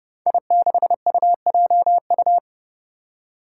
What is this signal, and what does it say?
Morse code, heard as a keyed tone, about 30 wpm, 700 Hz. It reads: I6UJU
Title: Five Challenging Letter-Number Combinations 30wpm